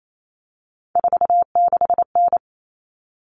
Morse code, heard as a keyed tone, about 28 wpm, 700 Hz. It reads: 46N
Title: All Cut Numbers - 3-Digit 28wpm